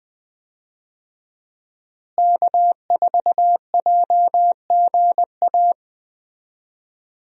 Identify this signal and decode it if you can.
Morse code, heard as a keyed tone, about 20 wpm, 700 Hz. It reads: K4JGA
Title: USA Call Signs - 5 Characters - Repeated 3x - 20wpm